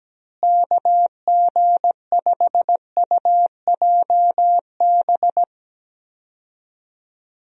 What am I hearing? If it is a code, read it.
Morse code, heard as a keyed tone, about 17 wpm, 700 Hz. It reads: KG5UJB
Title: USA Call Signs - 6 Characters - Repeated 3x - 17wpm